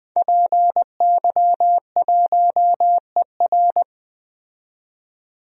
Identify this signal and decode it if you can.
Morse code, heard as a keyed tone, about 20 wpm, 700 Hz. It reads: PY1ER
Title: DXCC-100 Call Signs - 5 Characters - 20wpm